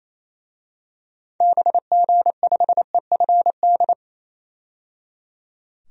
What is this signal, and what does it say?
Morse code, heard as a keyed tone, about 28 wpm, 700 Hz. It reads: BG5EFD
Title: DXCC-100 Call Signs - 6 Characters - 28wpm